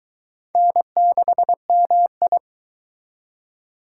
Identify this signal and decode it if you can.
Morse code, heard as a keyed tone, about 23 wpm, 700 Hz. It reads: N6MI